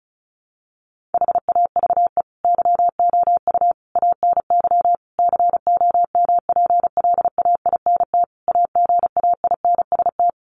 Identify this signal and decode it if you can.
Morse code, heard as a keyed tone, about 34 wpm, 700 Hz. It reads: HAVE YOU ANY COMPLAINT AGAINST
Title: Sentences from Top 1400 Words 35wpm